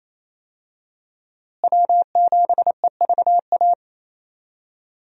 Morse code, heard as a keyed tone, about 28 wpm, 700 Hz. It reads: W7EVA
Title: USA Call Signs - 5 Characters - Repeated 3x - 28wpm